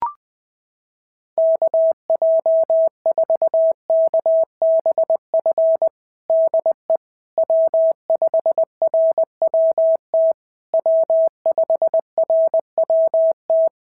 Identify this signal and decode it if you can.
Morse code, heard as a keyed tone, about 19 wpm, 650 Hz. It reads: KJ4KBF DE W5RWT W5RWT
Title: SOTA QSO Practice 20wpm